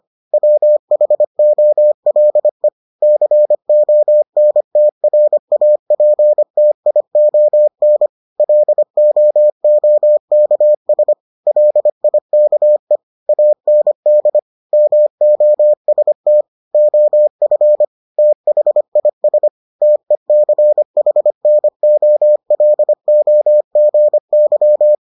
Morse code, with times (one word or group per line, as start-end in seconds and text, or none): 0.3-2.7 WHOLE
3.0-8.1 CONTRAPTION
8.4-11.1 LOOKS
11.5-13.0 LIKE
13.3-14.4 AND
14.7-16.4 MOST
16.7-17.8 OF
18.2-19.5 THIS
19.8-24.9 TECHNOLOGY